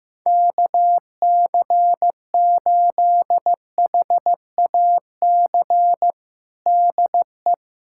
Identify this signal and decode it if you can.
Morse code, heard as a keyed tone, about 15 wpm, 700 Hz. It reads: KC8HAC DE
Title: USA Callsign DE Callsign - 6 Characters - 15wpm